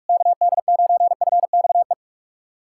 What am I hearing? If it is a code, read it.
Morse code, heard as a keyed tone, about 46 wpm, 700 Hz. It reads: KD9FXE
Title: SOTA QSO Practice 45wpm